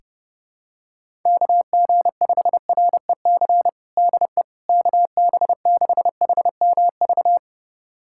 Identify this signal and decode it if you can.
Morse code, heard as a keyed tone, about 30 wpm, 700 Hz. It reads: KG5REC DE KB6HMV
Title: USA Callsign DE Callsign - 6 Characters - 30wpm